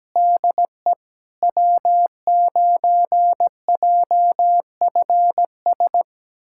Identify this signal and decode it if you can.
Morse code, heard as a keyed tone, about 17 wpm, 700 Hz. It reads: DE W9JFS
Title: USA Callsign DE Callsign - 5 Characters - 17wpm